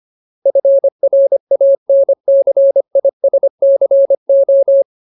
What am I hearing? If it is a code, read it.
Morse code, heard as a keyed tone, about 25 wpm, 550 Hz. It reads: FRANCISCO
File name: Ultimate Word List / 25WPM Ultimate